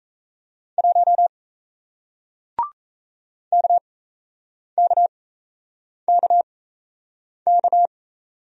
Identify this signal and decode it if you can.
Morse code, heard as a keyed tone, about 37 wpm, spaced as for 8 wpm, 700 Hz. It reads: 1 KKKK